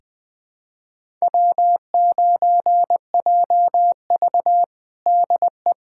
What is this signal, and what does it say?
Morse code, heard as a keyed tone, about 20 wpm, 700 Hz. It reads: W9JV DE
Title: USA Callsign DE Callsign - 4 Characters - 20wpm